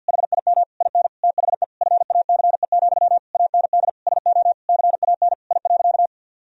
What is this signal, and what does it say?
Morse code, heard as a keyed tone, about 50 wpm, 700 Hz. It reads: HIM IN THE FACE, AND SO CAN I.
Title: Sentences from Top 1000 Words - Review Part 2 - 50wpm